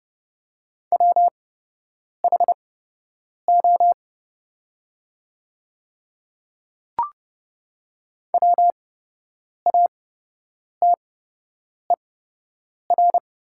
Morse code, heard as a keyed tone, about 30 wpm, spaced as for 8 wpm, 700 Hz. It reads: WHO WATER
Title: 8x Character Spacing - Top 100 Words - 30wpm